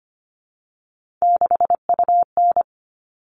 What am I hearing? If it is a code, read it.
Morse code, heard as a keyed tone, about 25 wpm, 700 Hz. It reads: 6UN
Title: All Cut Numbers - 3-Digit 25wpm